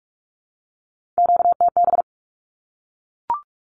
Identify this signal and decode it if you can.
Morse code, heard as a keyed tone, about 44 wpm, 700 Hz. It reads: QTB
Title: Q-Codes Repeated 3x 45wpm